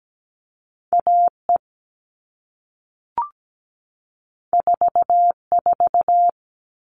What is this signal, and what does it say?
Morse code, heard as a keyed tone, about 17 wpm, 700 Hz. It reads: AE 44
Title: All Cut Numbers - 2-Digit 17wpm